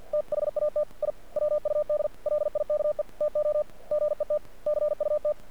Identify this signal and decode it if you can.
Morse code, heard as a keyed tone, about 48 wpm, 600 Hz. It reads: THAT I WUD LIKE TO GET CUT
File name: archived / CRSnet